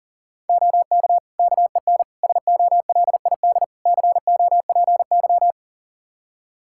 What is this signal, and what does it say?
Morse code, heard as a keyed tone, about 40 wpm, 700 Hz. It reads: OK KEN SOLID COPY
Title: Practice QSO - Exchange 3 - More Info - 40wpm